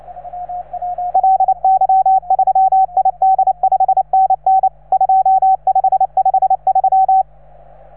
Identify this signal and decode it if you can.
Morse code, heard as a keyed tone, about 28 wpm, 750 Hz. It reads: LY3ID 5NN 2553